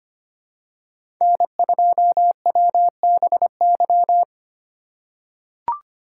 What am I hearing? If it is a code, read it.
Morse code, heard as a keyed tone, about 25 wpm, 700 Hz. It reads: N2WBY
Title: USA Call Signs - 5 Characters - 3x Character Spacing - 25wpm